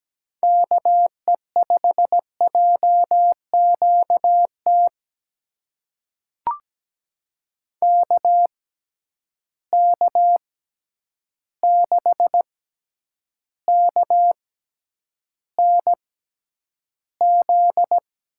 Morse code, heard as a keyed tone, about 17 wpm, spaced as for 9 wpm, 700 Hz. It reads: KE5JQT KK6KNZ
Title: USA Call Signs - 6 Characters - 6x Character Spacing - 17wpm